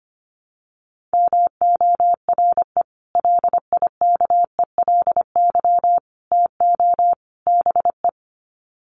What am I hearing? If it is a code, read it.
Morse code, heard as a keyed tone, about 25 wpm, 700 Hz. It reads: MORE LIKELY TO BE
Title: Sets of 4 Words 25wpm